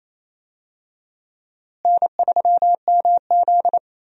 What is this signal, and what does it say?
Morse code, heard as a keyed tone, about 28 wpm, 700 Hz. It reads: N3MZ
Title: USA Call Signs - 4 Characters - Repeated 2x - 28wpm